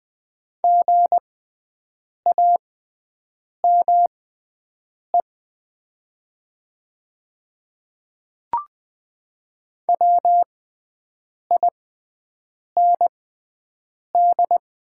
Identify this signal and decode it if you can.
Morse code, heard as a keyed tone, about 20 wpm, spaced as for 7 wpm, 700 Hz. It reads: GAME WIND
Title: 6x Character Spacing - Top 500 Words - 20wpm